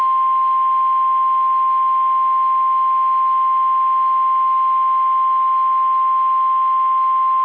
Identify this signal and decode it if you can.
Emergency Alert System (SAME) tones (the 1050 Hz attention tone). It is heard from 0.9 to 6.6 s.
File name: eas / KSDZKDJL / KSDZKDJL